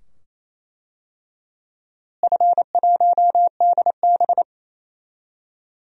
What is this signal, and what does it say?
Morse code, heard as a keyed tone, about 27 wpm, 700 Hz. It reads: F1DB
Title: DXCC-100 Call Signs - 4 Characters - Repeated 2x - 28wpm